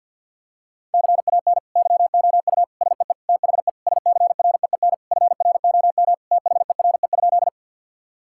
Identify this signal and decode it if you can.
Morse code, heard as a keyed tone, about 50 wpm, 700 Hz. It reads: CAN YOU SEE THE SCREEN FROM THERE?
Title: Sentences from Top 1000 Words - Review Part 6 - 50wpm